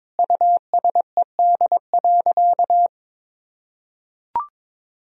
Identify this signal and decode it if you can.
Morse code, heard as a keyed tone, about 22 wpm, 700 Hz. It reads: USED.
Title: Sentences from Top 1000 Words - Review Part 4 - 22wpm